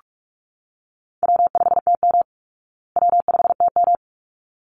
Morse code, heard as a keyed tone, about 44 wpm, 700 Hz. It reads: W5TM W5TM
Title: Call Signs Repeated 3x - 45wpm